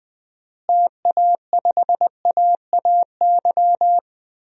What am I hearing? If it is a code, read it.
Morse code, heard as a keyed tone, about 21 wpm, 700 Hz. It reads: TA5AAY
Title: DXCC-100 Call Signs - 6 Characters - Repeated 2x - 20wpm